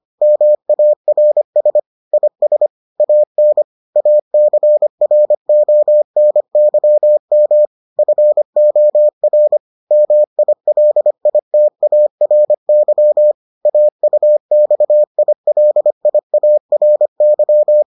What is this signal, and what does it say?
Morse code, heard as a keyed tone, about 25 wpm, 600 Hz. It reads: MARS IS AN ACRONYM FOR MILITARY AUXILIARY